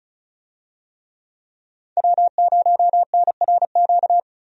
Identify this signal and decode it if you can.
Morse code, heard as a keyed tone, about 34 wpm, 700 Hz. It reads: W0NRQ
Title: USA Call Signs - 5 Characters - 35wpm